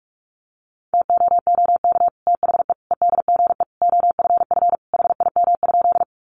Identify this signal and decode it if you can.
Morse code, heard as a keyed tone, about 44 wpm, 700 Hz. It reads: TOOK THE EDGE OFF HIM?
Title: Sentences from Top 1300 Words 45wpm